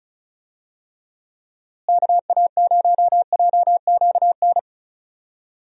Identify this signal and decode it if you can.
Morse code, heard as a keyed tone, about 36 wpm, 700 Hz. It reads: KA0JQN
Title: USA Call Signs - 6 Characters - Repeated 3x - 35wpm